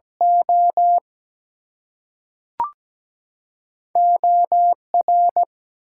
Morse code, heard as a keyed tone, about 17 wpm, 700 Hz. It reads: O OR
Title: Two Letter Words - 17wpm